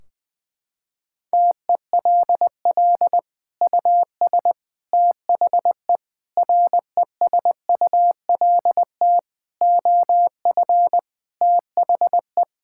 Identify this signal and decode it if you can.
Morse code, heard as a keyed tone, about 20 wpm, 700 Hz. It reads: TELL US THE RESULT OF THE
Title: Sentences from Top 1600 Words 20wpm